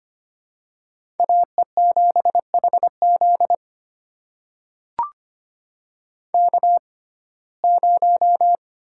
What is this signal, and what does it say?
Morse code, heard as a keyed tone, about 25 wpm, spaced as for 18 wpm, 700 Hz. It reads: AE7HZ K0
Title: USA Call Signs - 6x Character Spacing - 25wpm